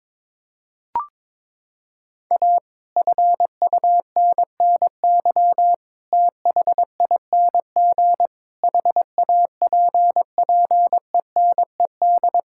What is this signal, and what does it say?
Morse code, heard as a keyed tone, about 22 wpm, 700 Hz. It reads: A FUNNY THING HAPPENED
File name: Sets of 9 Words - 22wpm